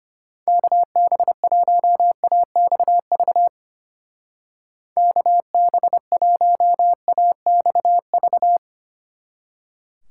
Morse code, heard as a keyed tone, about 32 wpm, 700 Hz. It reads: KB1AXV KB1AXV